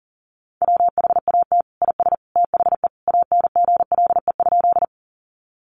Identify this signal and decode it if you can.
Morse code, heard as a keyed tone, about 40 wpm, 700 Hz. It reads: WHAT IS THE ANGLE?
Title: Sentences from Top 1500 Words 40wpm